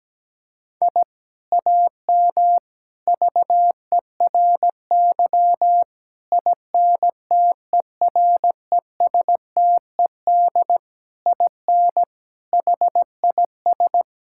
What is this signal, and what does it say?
Morse code, heard as a keyed tone, about 17 wpm, 700 Hz. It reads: I AM VERY INTERESTED IN HIS
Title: Sentences from Top 1000 Words - Review Part 6 - 17wpm